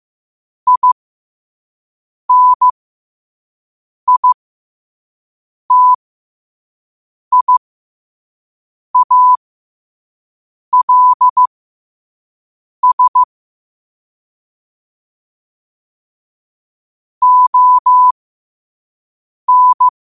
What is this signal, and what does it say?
Morse code, heard as a keyed tone, about 15 wpm, spaced as for 5 wpm, 1000 Hz. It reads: INITIALS ON
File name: qotdfiles / 5wpm